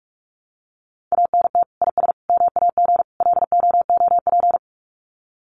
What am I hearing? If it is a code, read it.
Morse code, heard as a keyed tone, about 44 wpm, 700 Hz. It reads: ANT IS MAG LOOP
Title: Common QSO Phrases 45wpm